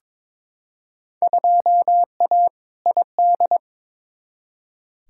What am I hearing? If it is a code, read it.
Morse code, heard as a keyed tone, about 21 wpm, 700 Hz. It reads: 2A ID
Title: Field Day Exchanges Encore - 22wpm